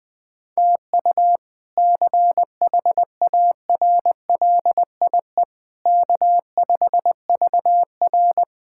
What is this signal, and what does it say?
Morse code, heard as a keyed tone, about 20 wpm, 700 Hz. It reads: TU CHARLIE K5VR
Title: K1USN SST QSO Practice - 20wpm